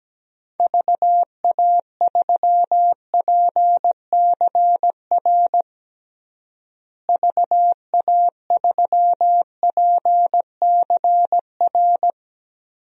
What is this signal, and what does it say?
Morse code, heard as a keyed tone, about 17 wpm, 700 Hz. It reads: VA3PCR VA3PCR
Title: DXCC-100 Call Signs - 6 Characters - Repeated 3x - 17wpm